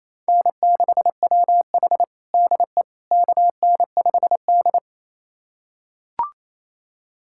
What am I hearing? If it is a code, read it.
Morse code, heard as a keyed tone, about 28 wpm, 700 Hz. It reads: N6WH DE KN5D
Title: USA Callsign DE Callsign - 4 Characters - 28wpm